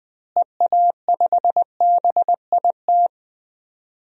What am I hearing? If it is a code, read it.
Morse code, heard as a keyed tone, about 20 wpm, 700 Hz. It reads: EA5BIT
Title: DXCC-100 Call Signs - 6 Characters - Repeated 2x - 20wpm